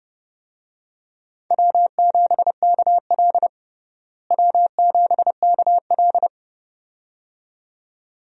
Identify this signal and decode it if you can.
Morse code, heard as a keyed tone, about 30 wpm, 700 Hz. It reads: W7KL W7KL
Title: USA Call Signs - 4 Characters - Repeated 2x - 30wpm